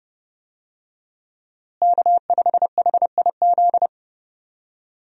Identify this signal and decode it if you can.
Morse code, heard as a keyed tone, about 30 wpm, 700 Hz. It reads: K5HIZ
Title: USA Call Signs - 5 Characters - Repeated 2x - 30wpm